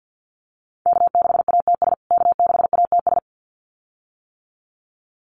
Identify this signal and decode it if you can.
Morse code, heard as a keyed tone, about 50 wpm, 700 Hz. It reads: K6ATS K6ATS
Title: USA Call Signs Repeated 2x 50wpm